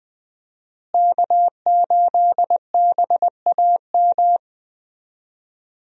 Morse code, heard as a keyed tone, about 20 wpm, 700 Hz. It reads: K8BAM
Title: USA Call Signs - 5 Characters - Repeated 2x - 20wpm